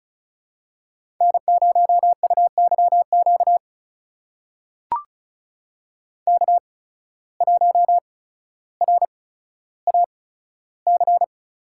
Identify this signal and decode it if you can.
Morse code, heard as a keyed tone, about 34 wpm, spaced as for 15 wpm, 700 Hz. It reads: N0UYQ K1RAC
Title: USA Call Signs - 5 Characters - 8x Character Spacing - 35wpm